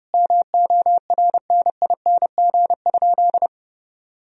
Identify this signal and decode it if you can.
Morse code, heard as a keyed tone, about 30 wpm, 700 Hz. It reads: MORNING?
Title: Sentences from Top 1000 Words - Review Part 3 - 30wpm